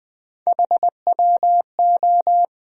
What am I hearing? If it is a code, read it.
Morse code, heard as a keyed tone, about 20 wpm, 700 Hz. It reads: HWO
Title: USA Call Sign Postfixes - 3 Letters 20wpm